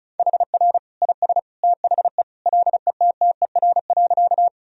Morse code, heard as a keyed tone, about 34 wpm, 700 Hz. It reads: HR IS THE LETTER.
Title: Sentences from Top 300 Words and Abbreviations 35wpm